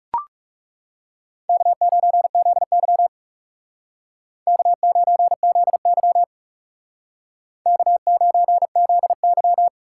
Morse code, heard as a keyed tone, about 38 wpm, 700 Hz. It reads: K9ZY K9ZY K9ZY